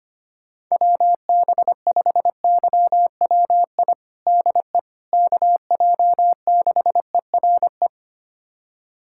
Morse code, heard as a keyed tone, about 25 wpm, 700 Hz. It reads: WB5YWI DE KJ6ERE
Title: USA Callsign DE Callsign - 6 Characters - 25wpm